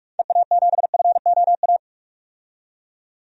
Morse code, heard as a keyed tone, about 45 wpm, 700 Hz. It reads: EA7FOA
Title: DXCC-100 Call Signs - 6 Characters - 45wpm